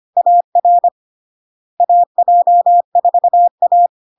Morse code, heard as a keyed tone, about 25 wpm, 700 Hz. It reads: AR AJ4A
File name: Log-in-CW-25wpm-700-Hz